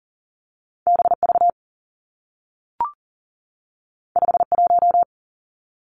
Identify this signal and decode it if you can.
Morse code, heard as a keyed tone, about 40 wpm, 700 Hz. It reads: B V 5 1
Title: Two Challenging Letter-Number Combinations 40wpm